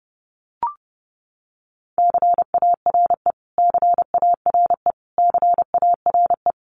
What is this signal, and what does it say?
Morse code, heard as a keyed tone, about 30 wpm, 700 Hz. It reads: CARE CARE CARE
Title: Top 500 Words Repeated 30wpm